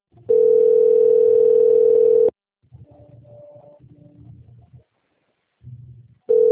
Ringback tone (North America), 440 + 480 Hz.